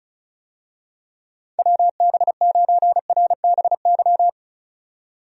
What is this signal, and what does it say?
Morse code, heard as a keyed tone, about 34 wpm, 700 Hz. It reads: WB9RBY
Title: USA Call Signs - 6 Characters - 35wpm